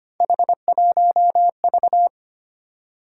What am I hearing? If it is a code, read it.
Morse code, heard as a keyed tone, about 25 wpm, 700 Hz. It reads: H1V
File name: Three Letter-Number Combinations - 25wpm